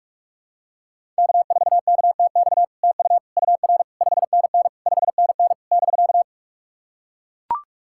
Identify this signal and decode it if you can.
Morse code, heard as a keyed tone, about 44 wpm, 690 Hz. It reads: K4KTX TU UR 5NN 5NN <BK>
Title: SOTA QSO Practice 45wpm